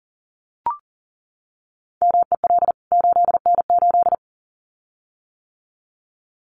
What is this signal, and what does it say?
Morse code, heard as a keyed tone, about 40 wpm, 700 Hz. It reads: MEL 8N8
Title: CWT Exchange 40wpm